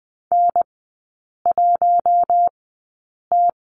Morse code, heard as a keyed tone, about 20 wpm, spaced as for 8 wpm, 700 Hz. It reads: N1T
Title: 041 Lesson 8e - Number 1 - Three Characters 20wpm